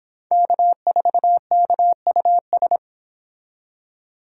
Morse code, heard as a keyed tone, about 28 wpm, 700 Hz. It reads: K4KUS